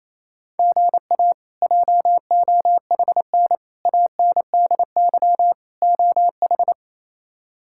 Morse code, heard as a keyed tone, about 27 wpm, 700 Hz. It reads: GA JOHN ANDY OH
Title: K1USN SST QSO Practice - 28wpm